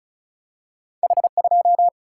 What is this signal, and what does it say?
Morse code, heard as a keyed tone, about 36 wpm, 700 Hz. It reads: H2
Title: Single-Letter Single-Number 35wpm